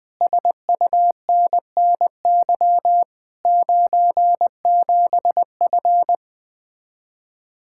Morse code, heard as a keyed tone, about 20 wpm, 700 Hz. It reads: SUNNY 97F
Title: Speed-Racing Common QSO Phrases - 17wpm